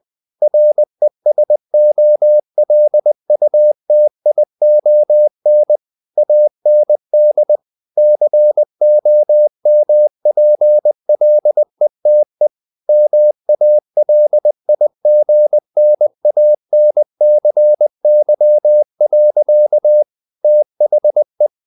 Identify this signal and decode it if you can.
Morse code, heard as a keyed tone, about 20 wpm, 600 Hz. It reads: RESOLUTION AND COMPLETE MALIGNANCY. THE
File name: the old man and the sea part 2 20wpm